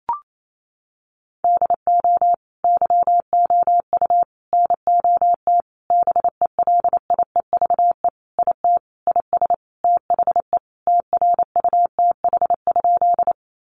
Morse code, heard as a keyed tone, about 28 wpm, 700 Hz. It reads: DO YOU NOT BELIEVE IT IS THE TRUTH?
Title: Sentences from Top 1000 Words 28wpm